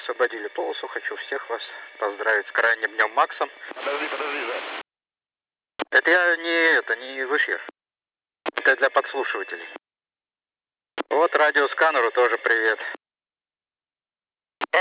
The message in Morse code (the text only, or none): none